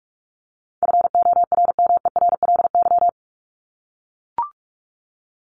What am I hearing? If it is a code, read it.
Morse code, heard as a keyed tone, about 44 wpm, 700 Hz. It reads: FORMERLY
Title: LY Suffix - Top 100 Words with Base Words 45wpm